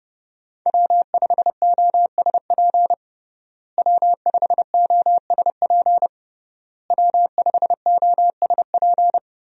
Morse code, heard as a keyed tone, about 30 wpm, 700 Hz. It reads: W5OSP W5OSP W5OSP
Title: USA Call Signs - 5 Characters - Repeated 3x - 30wpm